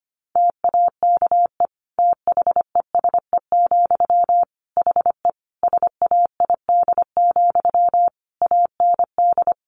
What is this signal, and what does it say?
Morse code, heard as a keyed tone, about 25 wpm, 700 Hz. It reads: TAKE THESE, HE SAID, AND
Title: Sentences from Top 300 Words 25wpm